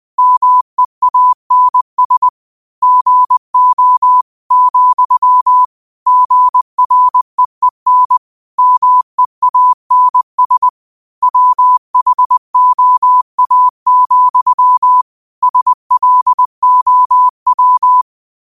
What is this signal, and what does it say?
Morse code, heard as a keyed tone, about 20 wpm, 1000 Hz. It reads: MEANS GO, GREEN MEANS WHOA, SLOW
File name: qotdfiles / 20wpm